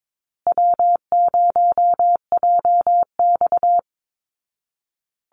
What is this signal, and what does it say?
Morse code, heard as a keyed tone, about 22 wpm, 700 Hz. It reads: W0JX
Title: Easy Call Signs Repeated 3x - 22wpm